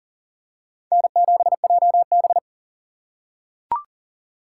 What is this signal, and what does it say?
Morse code, heard as a keyed tone, about 40 wpm, 700 Hz. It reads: N7JB
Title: USA Call Signs - 4 Characters - 6x Character Spacing - 40wpm